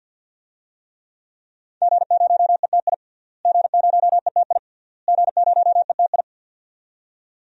Morse code, heard as a keyed tone, about 46 wpm, 700 Hz. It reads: G0ETI G0ETI G0ETI
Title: DXCC-100 Call Signs - 5 Characters - Repeated 3x - 50wpm